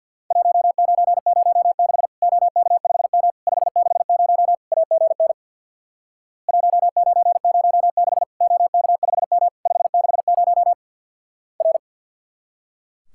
Morse code, heard as a keyed tone, about 50 wpm, 700 Hz. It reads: 1906 OK5M 560 AGN 1906 OK5M 560 R
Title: Worked All Europe DX Contest - QTC Exchange with Repeats - 50wpm